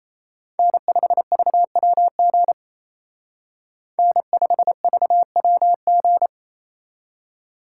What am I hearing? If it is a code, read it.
Morse code, heard as a keyed tone, about 32 wpm, 700 Hz. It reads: N5VWG N5VWG
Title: Speed Racing - USA Call Signs 28wpm